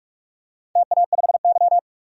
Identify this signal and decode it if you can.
Morse code, heard as a keyed tone, about 46 wpm, 700 Hz. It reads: TA5Y
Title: DXCC-100 Call Signs - 4 Characters - Repeated 2x - 45wpm